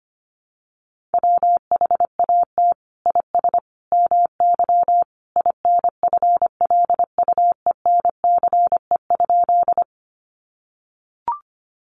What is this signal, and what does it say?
Morse code, heard as a keyed tone, about 25 wpm, 700 Hz. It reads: WHAT IS MY INFLUENCE?
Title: Sentences from Top 1100 Words 25wpm